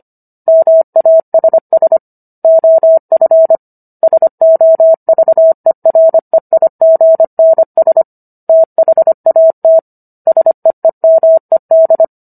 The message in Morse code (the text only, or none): MASS OF SOVEREIGNS THAT SEEMED